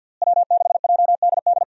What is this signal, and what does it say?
Morse code, heard as a keyed tone, about 52 wpm, 700 Hz. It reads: W6JDD
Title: USA Call Signs - 5 Characters - Repeated 2x - 50wpm